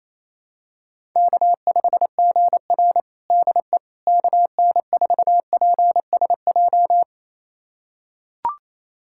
Morse code, heard as a keyed tone, about 28 wpm, 700 Hz. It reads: K5GR DE KN4PSJ
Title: USA Callsign DE Callsign 28wpm